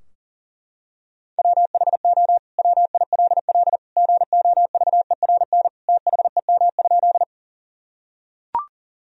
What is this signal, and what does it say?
Morse code, heard as a keyed tone, about 40 wpm, 700 Hz. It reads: WHO WILL GOVERN THEM?
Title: Sentences from Top 1000 Words - Review Part 2 - 40wpm